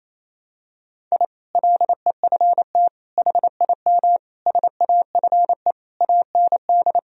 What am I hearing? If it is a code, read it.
Morse code, heard as a keyed tone, about 28 wpm, 700 Hz. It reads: I LEFT HIM SAFE AND
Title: Sentences from Top 1000 Words - Review Part 1 - 28wpm